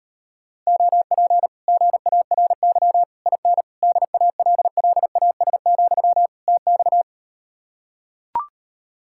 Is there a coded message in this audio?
Yes: Morse code from 0.7 to 7.0 s, heard as a keyed tone, about 38 wpm, 700 Hz. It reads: OP GARY IN DALLAS, TX